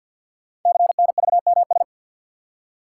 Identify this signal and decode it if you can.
Morse code, heard as a keyed tone, about 48 wpm, 690 Hz. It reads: CNVMS
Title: Code Groups - 5 Letters - 50wpm